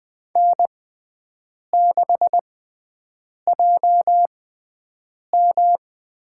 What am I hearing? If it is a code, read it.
Morse code, heard as a keyed tone, about 20 wpm, spaced as for 7 wpm, 700 Hz. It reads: N6JM
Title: USA Call Signs - 4 Characters - 6x Character Spacing - 20wpm